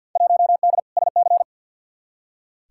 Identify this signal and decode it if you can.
Morse code, heard as a keyed tone, about 47 wpm, 700 Hz. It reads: 1D SC
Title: Field Day Exchanges - 50wpm